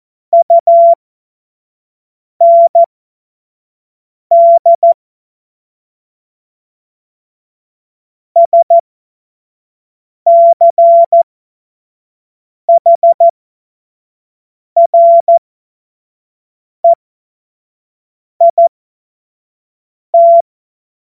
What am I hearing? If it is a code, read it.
Morse code, heard as a keyed tone, about 14 wpm, spaced as for 5 wpm, 680 Hz. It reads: UND SCHREIT
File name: Morsekurs / Witze / Witz 21. Dezember 2021, 25 F70